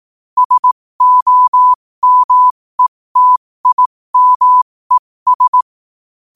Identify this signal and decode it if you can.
Morse code, heard as a keyed tone, about 18 wpm, spaced as for 15 wpm, 1000 Hz. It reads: SOMETIMES
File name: qotdfiles / 15wpm